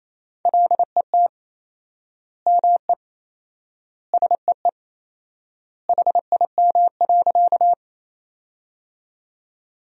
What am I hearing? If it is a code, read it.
Morse code, heard as a keyed tone, about 28 wpm, 700 Hz. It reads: LET ME SEE HIM.
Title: 4x Word-Spacing - Sentences from Top 300 Words - 28wpm